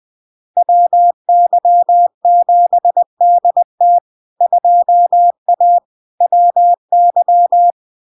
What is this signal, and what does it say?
Morse code, heard as a keyed tone, about 20 wpm, 700 Hz. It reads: WY7DT 2A WY
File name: LARK-FD-2022-Log-in-CW-20wpm-700-Hz